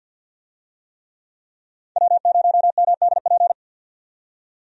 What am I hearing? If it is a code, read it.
Morse code, heard as a keyed tone, about 48 wpm, 700 Hz. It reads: W0MDP
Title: USA Call Signs - 5 Characters - 50wpm